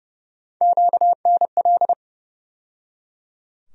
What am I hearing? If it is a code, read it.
Morse code, heard as a keyed tone, about 31 wpm, 700 Hz. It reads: QNL
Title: USA Call Sign Postfixes - 3 Letters 30wpm